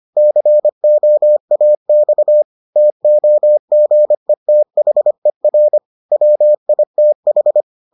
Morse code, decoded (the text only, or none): COAX TOGETHER WITH